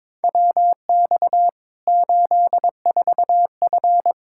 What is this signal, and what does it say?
Morse code, heard as a keyed tone, about 21 wpm, 700 Hz. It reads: WX 84F
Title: Practice QSO - Exchange 2 - Info - 22wpm